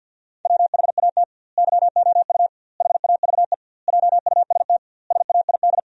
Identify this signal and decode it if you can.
Morse code, heard as a keyed tone, about 52 wpm, 700 Hz. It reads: WHAT YOU HAVE JUST SAID
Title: 2x Word-Spacing - Sentences from Top 200 Words - 50wpm